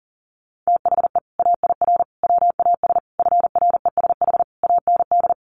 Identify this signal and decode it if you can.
Morse code, heard as a keyed tone, about 40 wpm, 700 Hz. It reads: THE AIR WAS FRESH AND
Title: Sentences from Top 1200 Words 40wpm